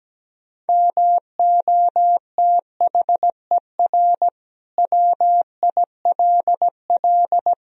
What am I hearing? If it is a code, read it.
Morse code, heard as a keyed tone, about 17 wpm, 700 Hz. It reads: MOTHER WILL
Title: Sentences from Top 1000 Words - Review Part 2 - 17wpm